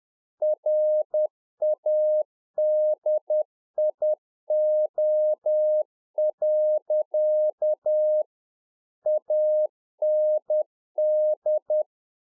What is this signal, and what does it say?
Morse code, heard as a keyed tone, about 10 wpm, 610 Hz. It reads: RADIO. AND